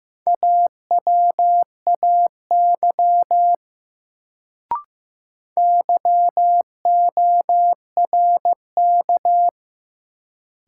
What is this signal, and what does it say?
Morse code, heard as a keyed tone, about 15 wpm, 700 Hz. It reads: AWAY YORK
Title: Sets of 2 Words - New Word Drill - 15wpm